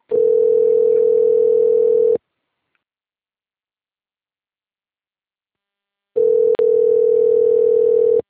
Ringback tone (North America), 440 + 480 Hz.